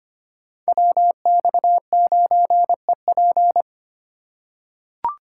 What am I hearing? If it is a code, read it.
Morse code, heard as a keyed tone, about 25 wpm, 700 Hz. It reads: WX9EP
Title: USA Call Signs - 5 Characters - 4x Character Spacing - 25wpm